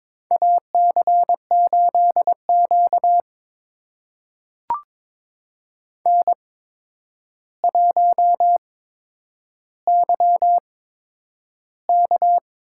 Morse code, heard as a keyed tone, about 22 wpm, spaced as for 9 wpm, 700 Hz. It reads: AC8Q N1YK
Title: USA Call Signs - 8x Character Spacing - 22wpm